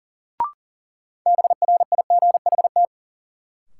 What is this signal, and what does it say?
Morse code, heard as a keyed tone, about 40 wpm, 700 Hz. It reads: BRIGHT
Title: Sentences from Top 1000 Words - New Word Drill - 40wpm